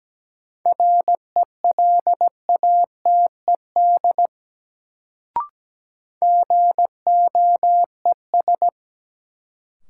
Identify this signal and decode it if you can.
Morse code, heard as a keyed tone, about 17 wpm, 700 Hz. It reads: RELATED GOES
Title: Sentences from Top 1000 Words - Review Part 5 - New Word Drill - 17wpm